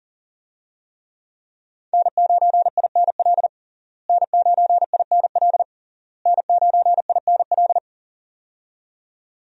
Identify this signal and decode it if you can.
Morse code, heard as a keyed tone, about 40 wpm, 700 Hz. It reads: N9INL N9INL N9INL
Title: USA Call Signs - 5 Characters - Repeated 3x - 40wpm